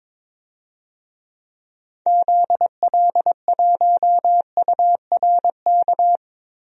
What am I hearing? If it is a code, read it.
Morse code, heard as a keyed tone, about 21 wpm, 700 Hz. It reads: ZL1URK
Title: DXCC-100 Call Signs - 6 Characters - Repeated 3x - 22wpm